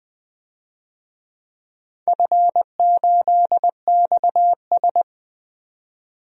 Morse code, heard as a keyed tone, about 20 wpm, 700 Hz. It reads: F8XS
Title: DXCC-100 Call Signs - 4 Characters - 20wpm